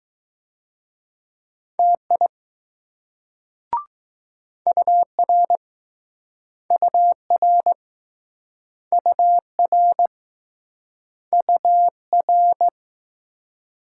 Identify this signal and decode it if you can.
Morse code, heard as a keyed tone, about 20 wpm, 700 Hz. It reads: TI UR UR UR UR